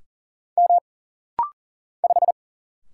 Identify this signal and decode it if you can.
Morse code, heard as a keyed tone, about 40 wpm, 700 Hz. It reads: M 5
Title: Single Letter-Number - Rapid-Fire 40wpm